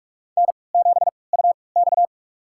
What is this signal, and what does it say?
Morse code, heard as a keyed tone, about 44 wpm, spaced as for 25 wpm, 690 Hz. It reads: N7UX
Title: USA Call Signs - 4 Characters - 3x Character Spacing - 45wpm